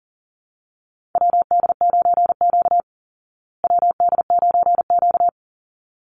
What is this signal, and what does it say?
Morse code, heard as a keyed tone, about 40 wpm, 700 Hz. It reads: WD9Q WD9Q
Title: Call Signs Repeated 3x - 40wpm